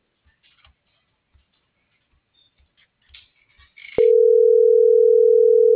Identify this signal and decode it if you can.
Ringback tone (North America), 440 + 480 Hz.